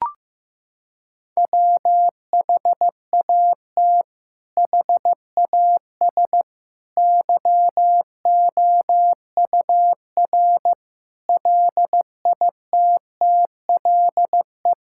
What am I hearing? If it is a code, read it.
Morse code, heard as a keyed tone, about 15 wpm, 700 Hz. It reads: WHAT HAS YOUR LITTLE
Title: Sentences from Top 1600 Words 15wpm